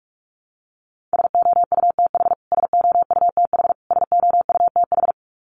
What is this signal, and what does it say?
Morse code, heard as a keyed tone, about 44 wpm, 700 Hz. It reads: SOUTH SOUTH SOUTH
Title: Top 500 Words Repeated 45wpm